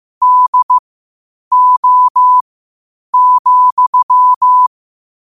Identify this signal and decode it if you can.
Morse code, heard as a keyed tone, about 15 wpm, spaced as for 8 wpm, 1000 Hz. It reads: DO,
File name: qotdfiles / 8wpm